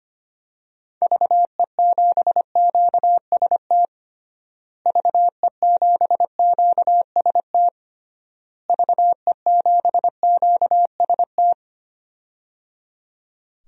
Morse code, heard as a keyed tone, about 25 wpm, 700 Hz. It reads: VE7QST VE7QST VE7QST
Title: DXCC-100 Call Signs - 6 Characters - Repeated 3x - 25wpm